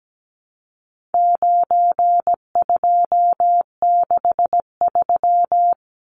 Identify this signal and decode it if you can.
Morse code, heard as a keyed tone, about 17 wpm, 700 Hz. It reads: 9263
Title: Numbers - 4-Digit 17wpm